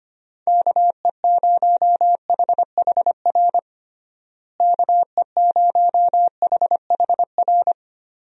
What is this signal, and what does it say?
Morse code, heard as a keyed tone, about 25 wpm, 700 Hz. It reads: KE0HHR KE0HHR
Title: USA Call Signs - 6 Characters - Repeated 3x - 25wpm